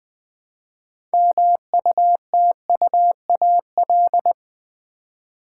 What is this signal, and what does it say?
Morse code, heard as a keyed tone, about 20 wpm, 700 Hz. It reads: MUTUAL
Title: Build A Word - 2 Letters At A Time - 6-Letter Words - 20wpm